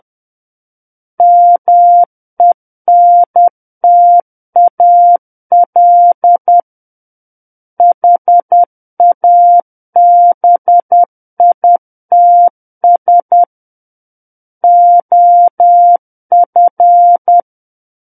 Morse code, heard as a keyed tone, about 10 wpm, 700 Hz. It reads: MENTAL HABITS OF
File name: Book One / 01-Chapter 1 - 10 WPM